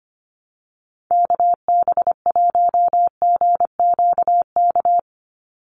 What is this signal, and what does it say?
Morse code, heard as a keyed tone, about 25 wpm, 700 Hz. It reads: KB1GQK
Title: USA Call Signs 25wpm